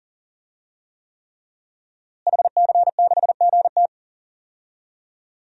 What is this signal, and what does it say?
Morse code, heard as a keyed tone, about 40 wpm, 700 Hz. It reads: HC6GT
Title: DXCC-100 Call Signs - 5 Characters - Repeated 3x - 40wpm